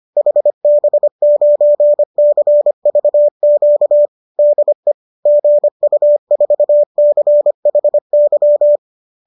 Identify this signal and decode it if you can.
Morse code, heard as a keyed tone, about 25 wpm, 570 Hz. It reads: HB9CVQ DE GU4CHY